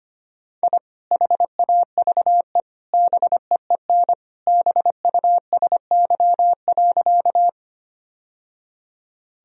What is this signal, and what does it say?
Morse code, heard as a keyed tone, about 25 wpm, 700 Hz. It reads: I HAVE BEEN BUSY.
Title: Sentences from Top 1000 Words - Review Part 3 - 25wpm